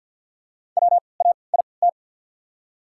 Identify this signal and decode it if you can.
Morse code, heard as a keyed tone, about 53 wpm, spaced as for 29 wpm, 690 Hz. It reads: WAIT
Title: 3x Character Spacing - Top 500 Words - 50wpm